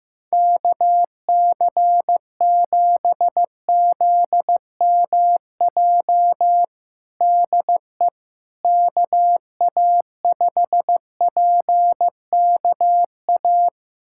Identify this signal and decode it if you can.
Morse code, heard as a keyed tone, about 15 wpm, 700 Hz. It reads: KC7ZMJ DE KA5PKA
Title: USA Callsign DE Callsign - 6 Characters - 15wpm